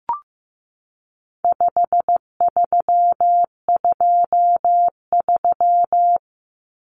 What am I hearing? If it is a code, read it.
Morse code, heard as a keyed tone, about 15 wpm, 700 Hz. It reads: 5323
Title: Numbers - 4-Digit 15wpm